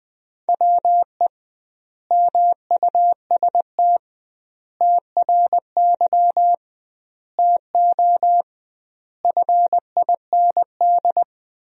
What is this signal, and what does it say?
Morse code, heard as a keyed tone, about 21 wpm, 700 Hz. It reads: WE MUST TRY TO FIND
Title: 2x Word-Spacing - Sentences from Top 400 Words - 20wpm